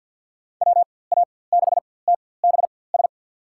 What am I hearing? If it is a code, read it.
Morse code, heard as a keyed tone, about 52 wpm, spaced as for 23 wpm, 700 Hz. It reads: WA6TBS
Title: USA Call Signs - 6 Characters - 4x Character Spacing - 50wpm